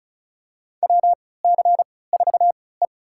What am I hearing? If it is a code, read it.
Morse code, heard as a keyed tone, about 36 wpm, spaced as for 20 wpm, 700 Hz. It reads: WC4E
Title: USA Call Signs - 4 Characters - 3x Character Spacing - 35wpm